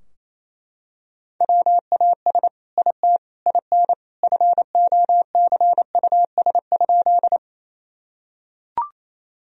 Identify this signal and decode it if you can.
Morse code, heard as a keyed tone, about 28 wpm, 700 Hz. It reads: WAS IT IN FOCUS?
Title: Sentences from Top 1000 Words - Review Part 4 - 28wpm